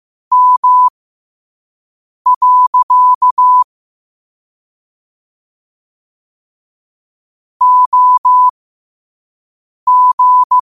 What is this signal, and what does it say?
Morse code, heard as a keyed tone, about 15 wpm, spaced as for 5 wpm, 1000 Hz. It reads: M. OG
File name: qotdfiles / 5wpm